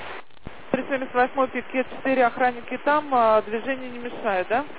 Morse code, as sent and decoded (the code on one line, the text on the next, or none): none